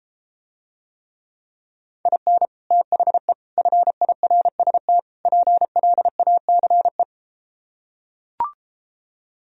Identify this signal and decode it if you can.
Morse code, heard as a keyed tone, about 33 wpm, 700 Hz. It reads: IN THE FIRST PLACE